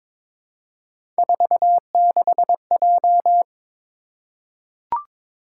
Morse code, heard as a keyed tone, about 22 wpm, 700 Hz. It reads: 46J
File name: practice / 22wpm / Three Letter-Number Combinations - 22wpm